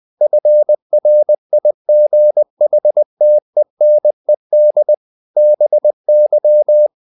Morse code, heard as a keyed tone, about 20 wpm, 600 Hz. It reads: FRIGHTENED BY
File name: the old man and the sea part 2 20wpm